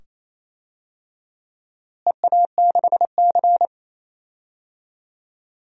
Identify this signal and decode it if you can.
Morse code, heard as a keyed tone, about 28 wpm, 700 Hz. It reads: EA6C
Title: DXCC-100 Call Signs - 4 Characters - Repeated 2x - 28wpm